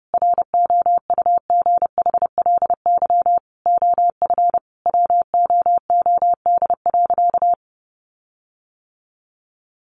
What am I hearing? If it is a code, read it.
Morse code, heard as a keyed tone, about 30 wpm, 700 Hz. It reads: ROUGHLY OF WOOD.
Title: Sentences from Top 1500 Words 30wpm